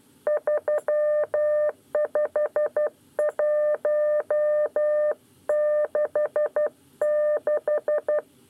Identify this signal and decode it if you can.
Morse code, heard as a keyed tone, about 12 wpm, 600 Hz. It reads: 35166
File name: 9 lezione cw 2